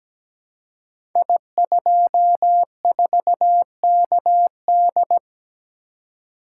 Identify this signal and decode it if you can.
Morse code, heard as a keyed tone, about 17 wpm, 700 Hz. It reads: I24KD
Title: Code Groups - 5 Letter:Numbers - 17wpm